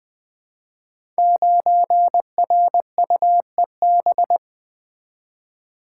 Morse code, heard as a keyed tone, about 20 wpm, 700 Hz. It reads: 9RUEB
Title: Code Groups - 5 Letter:Numbers - 20wpm